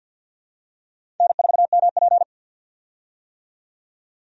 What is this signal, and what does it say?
Morse code, heard as a keyed tone, about 52 wpm, 700 Hz. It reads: N4MP
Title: USA Call Signs - 4 Characters - 50wpm